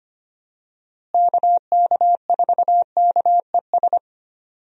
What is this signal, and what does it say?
Morse code, heard as a keyed tone, about 25 wpm, 700 Hz. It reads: KK4KES
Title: USA Call Signs - 6 Characters - Repeated 2x - 25wpm